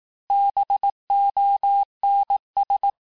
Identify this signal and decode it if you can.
Morse code, heard as a keyed tone, about 18 wpm, 800 Hz. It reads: BONS
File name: cours-du-lundi files / cours-1594--FIN-de-QSO-vitesse-17mpm